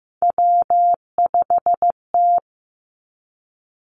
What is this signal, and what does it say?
Morse code, heard as a keyed tone, about 15 wpm, 700 Hz. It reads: W5T
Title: Three Challenging Letter-Number Combinations 15wpm